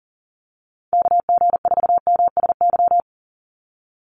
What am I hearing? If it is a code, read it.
Morse code, heard as a keyed tone, about 40 wpm, 700 Hz. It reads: KG4MSY
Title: USA Call Signs 40wpm